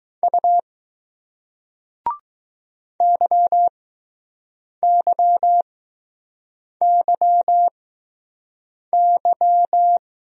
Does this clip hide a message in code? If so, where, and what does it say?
Yes: Morse code from 0.2 to 10.0 s, heard as a keyed tone, about 20 wpm, spaced as for 6 wpm, 700 Hz. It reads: U YYYY